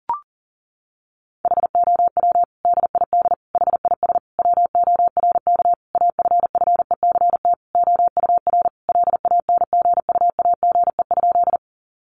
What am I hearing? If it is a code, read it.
Morse code, heard as a keyed tone, about 37 wpm, 700 Hz. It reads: HOW DID HIS WORK AFFECT OUR LANGUAGE?
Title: Sentences from Top 900 Words 40wpm